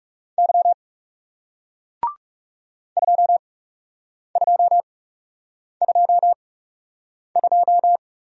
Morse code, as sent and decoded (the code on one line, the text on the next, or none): -.-- / ..--- ..--- ..--- ..---
Y 2222